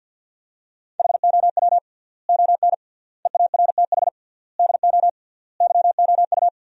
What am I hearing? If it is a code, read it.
Morse code, heard as a keyed tone, about 50 wpm, 700 Hz. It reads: HOW ON EARTH DO YOU
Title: 3x Word-Spacing - Sentences from Top 200 Words - 50wpm